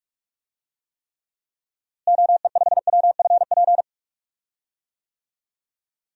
Morse code, heard as a keyed tone, about 46 wpm, 700 Hz. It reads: OE5WFP
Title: DXCC-100 Call Signs - 6 Characters - Repeated 2x - 45wpm